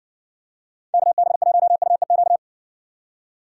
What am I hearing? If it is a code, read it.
Morse code, heard as a keyed tone, about 60 wpm, 700 Hz. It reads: KB1UEQ